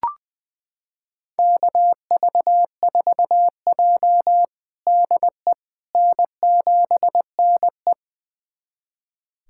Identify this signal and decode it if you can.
Morse code, heard as a keyed tone, about 21 wpm, 700 Hz. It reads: KV4J DE N7NE
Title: USA Callsign DE Callsign - 4 Characters - 20wpm